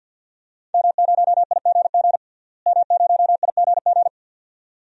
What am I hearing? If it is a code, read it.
Morse code, heard as a keyed tone, about 51 wpm, 700 Hz. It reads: M0IGG M0IGG
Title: DXCC-100 Call Signs - 5 Characters - Repeated 2x - 50wpm